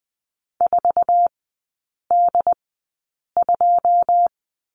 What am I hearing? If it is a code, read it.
Morse code, heard as a keyed tone, about 20 wpm, spaced as for 8 wpm, 700 Hz. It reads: 4D2
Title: 081 Lesson 14e - Number 2 - Three Characters 20wpm